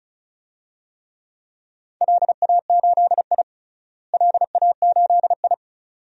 Morse code, heard as a keyed tone, about 34 wpm, 700 Hz. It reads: LA8I LA8I
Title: DXCC-100 Call Signs - 4 Characters - Repeated 2x - 35wpm